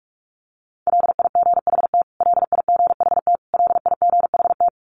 Morse code, heard as a keyed tone, about 44 wpm, 700 Hz. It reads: LIGHT LIGHT LIGHT
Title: Top 200 Words Repeated 45wpm